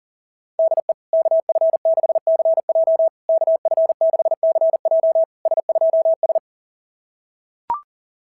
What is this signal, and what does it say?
Morse code, heard as a keyed tone, about 37 wpm, 690 Hz. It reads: DE KF6CJ KF6CJ S2S
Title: SOTA QSO Practice 40wpm